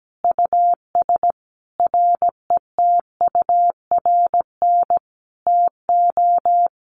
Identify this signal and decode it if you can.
Morse code, heard as a keyed tone, about 17 wpm, 700 Hz. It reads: US RETURN TO
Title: Sentences from Top 1500 Words 17wpm